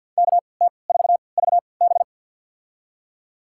Morse code, heard as a keyed tone, about 51 wpm, spaced as for 28 wpm, 700 Hz. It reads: KT4VB
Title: USA Call Signs - 5 Characters - 3x Character Spacing - 50wpm